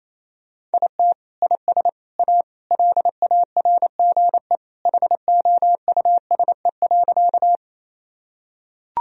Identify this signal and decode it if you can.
Morse code, heard as a keyed tone, about 28 wpm, 700 Hz. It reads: IT IS A LARGE HOUSE.
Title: Sentences from Top 1000 Words - Review Part 1 - 28wpm